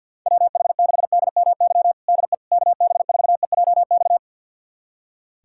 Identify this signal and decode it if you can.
Morse code, heard as a keyed tone, about 51 wpm, 700 Hz. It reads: WH6DMY DE KB4EJX
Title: USA Callsign DE Callsign - 6 Characters - 50wpm